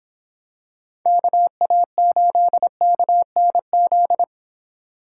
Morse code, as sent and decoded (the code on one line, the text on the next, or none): -.- .- ---.. -.- -. --..
KA8KNZ